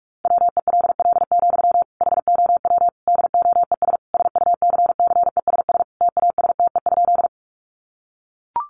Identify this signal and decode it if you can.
Morse code, heard as a keyed tone, about 44 wpm, 700 Hz. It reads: WELL, HOW DOES SUCCESS TASTE?
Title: Sentences from Top 1100 Words 45wpm